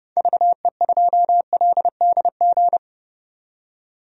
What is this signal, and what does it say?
Morse code, heard as a keyed tone, about 30 wpm, 700 Hz. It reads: VE2LDG
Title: DXCC-100 Call Signs - 6 Characters - 30wpm